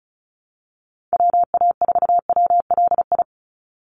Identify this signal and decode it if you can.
Morse code, heard as a keyed tone, about 34 wpm, 700 Hz. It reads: WA4WLI
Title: Call Signs Repeated 2x - 35wpm